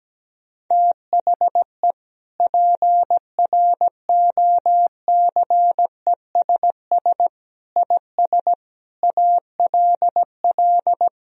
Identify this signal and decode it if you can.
Morse code, heard as a keyed tone, about 17 wpm, 700 Hz. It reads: THE PROCESS IS ALL
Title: Sentences from Top 1000 Words - Review Part 3 - 17wpm